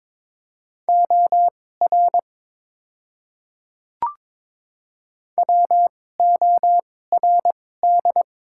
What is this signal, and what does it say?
Morse code, heard as a keyed tone, about 22 wpm, spaced as for 16 wpm, 700 Hz. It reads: OR WORD
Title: 2x Character Spacing - Top 200 Words - 22wpm